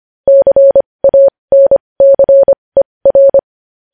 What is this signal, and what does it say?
Morse code, heard as a keyed tone, about 25 wpm, spaced as for 20 wpm, 550 Hz. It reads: CANCER
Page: news headlines in morse code at 20 wpm